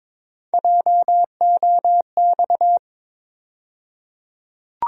Morse code, heard as a keyed tone, about 22 wpm, 700 Hz. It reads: JOX
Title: USA Call Sign Postfixes - 3 Letters 22wpm